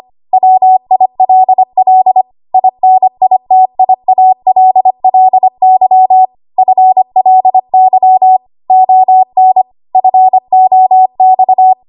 Morse code, heard as a keyed tone, about 25 wpm, 750 Hz. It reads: WILL INITIALLY FLY ON FOX
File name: Archive / 25 WPM / 25WPM